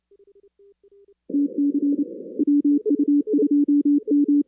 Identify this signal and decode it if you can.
Morse code, heard as a keyed tone, about 28 wpm, 290 Hz. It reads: TC WU2M
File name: 2016 arrl dx cw / 2016-ARRL-DX-CW-K5TR- - 0-41